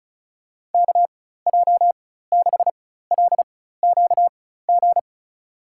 Morse code, heard as a keyed tone, about 36 wpm, spaced as for 16 wpm, 700 Hz. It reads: KJ6LQG
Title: USA Call Signs - 4x Character Spacing - 35wpm